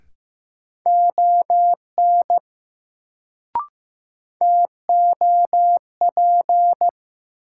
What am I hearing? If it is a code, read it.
Morse code, heard as a keyed tone, about 15 wpm, 700 Hz. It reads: ON TOP
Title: Sets of 2 Words - New Word Drill - 15wpm